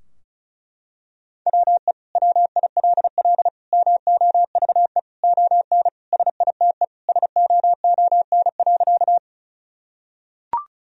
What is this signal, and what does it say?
Morse code, heard as a keyed tone, about 34 wpm, 700 Hz. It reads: WE WILL MOVE ON SITE SOON.
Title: Sentences from Top 1000 Words - Review Part 4 - 35wpm